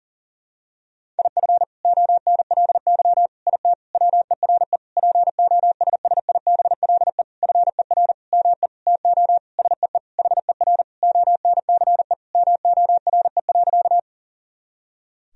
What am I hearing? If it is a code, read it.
Morse code, heard as a keyed tone, about 40 wpm, 700 Hz. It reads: IF ONLY IT WERE POSSIBLE FER ME TO SEE HER ONCE MORE.
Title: Sentences from Top 500 Words and Abbreviations 40wpm